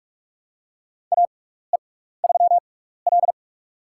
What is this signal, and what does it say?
Morse code, heard as a keyed tone, about 45 wpm, spaced as for 16 wpm, 700 Hz. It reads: AE3L
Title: USA Call Signs - 4 Characters - 6x Character Spacing - 45wpm